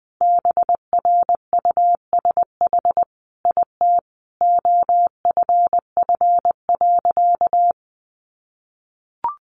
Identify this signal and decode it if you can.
Morse code, heard as a keyed tone, about 20 wpm, 700 Hz. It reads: BRUSH IT OFF.
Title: Sentences from Top 1100 Words 20wpm